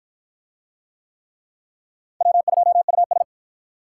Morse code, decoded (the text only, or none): W2US